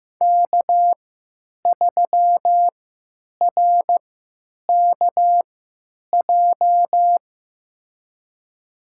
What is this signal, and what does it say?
Morse code, heard as a keyed tone, about 15 wpm, spaced as for 9 wpm, 700 Hz. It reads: K3RKJ
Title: USA Call Signs - 5 Characters - 3x Character Spacing - 15wpm